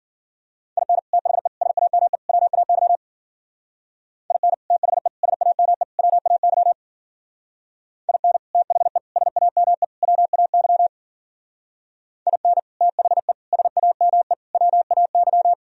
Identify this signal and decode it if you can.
Morse code, heard as a keyed tone, about 51 wpm, 700 Hz. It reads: IN THE SAME WAY IN THE SAME WAY IN THE SAME WAY IN THE SAME WAY